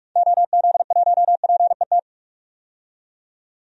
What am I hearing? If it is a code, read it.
Morse code, heard as a keyed tone, about 46 wpm, 700 Hz. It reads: OZ1PET
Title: DXCC-100 Call Signs - 6 Characters - 45wpm